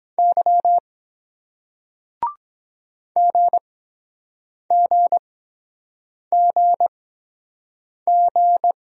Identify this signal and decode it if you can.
Morse code, heard as a keyed tone, about 23 wpm, spaced as for 7 wpm, 700 Hz. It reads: Y GGGG